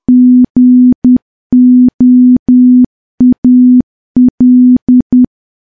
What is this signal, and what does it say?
Morse code, heard as a keyed tone, about 10 wpm, 250 Hz. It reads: GOAL
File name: morse / morse-training-10WPM